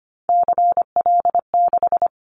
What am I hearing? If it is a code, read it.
Morse code, heard as a keyed tone, about 25 wpm, 700 Hz. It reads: CL6
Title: Callsign Prefixes for Top 100 DXCC - 25wpm